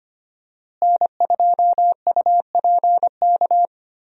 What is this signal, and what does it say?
Morse code, heard as a keyed tone, about 25 wpm, 700 Hz. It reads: N2UPK
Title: USA Call Signs - 5 Characters - Repeated 3x - 25wpm